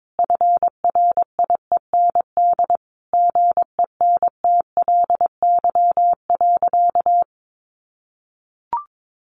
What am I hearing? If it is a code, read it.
Morse code, heard as a keyed tone, about 22 wpm, 700 Hz. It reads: FRIEND GENTLY.
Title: Sentences from Top 1500 Words 22wpm